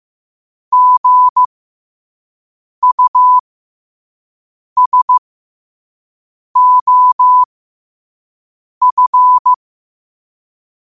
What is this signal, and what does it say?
Morse code, heard as a keyed tone, about 15 wpm, spaced as for 5 wpm, 1000 Hz. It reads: GUSOF
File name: qotdfiles / 5wpm